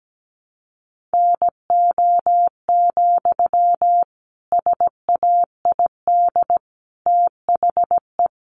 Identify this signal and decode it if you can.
Morse code, heard as a keyed tone, about 17 wpm, 700 Hz. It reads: NO, SAID THE
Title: Sentences from Top 1200 Words 17wpm